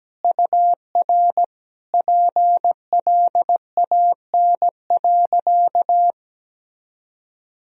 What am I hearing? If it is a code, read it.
Morse code, heard as a keyed tone, about 17 wpm, 700 Hz. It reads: UR PLAN.
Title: Sentences from Top 500 Words and Abbreviations 17wpm